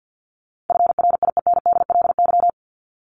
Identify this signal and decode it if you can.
Morse code, heard as a keyed tone, about 50 wpm, 700 Hz. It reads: FRIENDLY
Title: LY Suffix - Top 100 Words with Base Words 50wpm